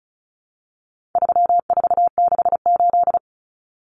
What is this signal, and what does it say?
Morse code, heard as a keyed tone, about 34 wpm, 700 Hz. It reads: 3468
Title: Numbers - 4-Digit 35wpm